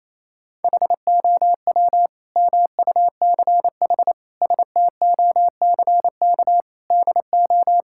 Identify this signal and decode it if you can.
Morse code, heard as a keyed tone, about 28 wpm, 700 Hz. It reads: HOW MUCH STOCK DO
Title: Sentences from Top 1000 Words - Review Part 5 - 28wpm